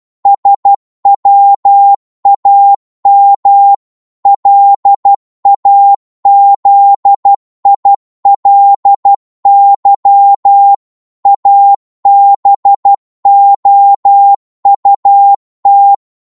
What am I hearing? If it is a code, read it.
Morse code, heard as a keyed tone, about 12 wpm, 800 Hz. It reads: SWAM LAZILY ABOUT
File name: Morse / psg-12wpm